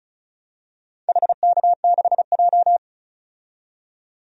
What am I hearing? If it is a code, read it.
Morse code, heard as a keyed tone, about 36 wpm, 700 Hz. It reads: HK6J
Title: DXCC-100 Call Signs - 4 Characters - 35wpm